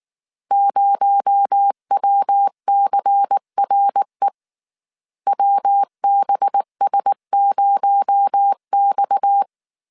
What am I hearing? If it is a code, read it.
Morse code, heard as a keyed tone, about 19 wpm, 800 Hz. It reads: 0WCRE WBS0X